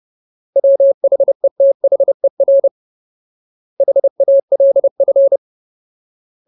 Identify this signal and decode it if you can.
Morse code, heard as a keyed tone, about 30 wpm, 550 Hz. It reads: WHETHER HALF
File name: Ultimate Word List / 30WPM Ultimate